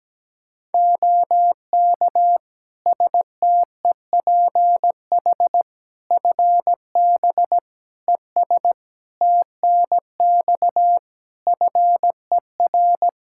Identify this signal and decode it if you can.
Morse code, heard as a keyed tone, about 17 wpm, 700 Hz. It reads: OK STEPH FB ES TNX FER
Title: Practice QSO - Exchange 2 - Info - 17wpm